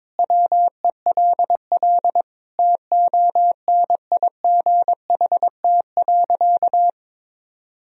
Morse code, heard as a keyed tone, about 22 wpm, 700 Hz. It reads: WELL TONIGHT.
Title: Sentences from Top 1000 Words - Review Part 4 - 22wpm